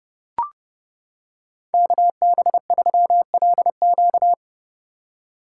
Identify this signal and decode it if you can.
Morse code, heard as a keyed tone, about 31 wpm, 700 Hz. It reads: KB3LQ